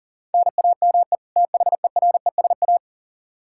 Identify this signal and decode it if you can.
Morse code, heard as a keyed tone, about 40 wpm, 700 Hz. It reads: NAME THERESA
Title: K1USN SST QSO Practice 40wpm